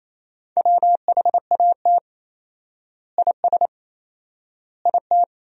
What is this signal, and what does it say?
Morse code, heard as a keyed tone, about 27 wpm, 700 Hz. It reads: WHAT IS IT
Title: 4x Word-Spacing - Sentences from Top 100 Words - 28wpm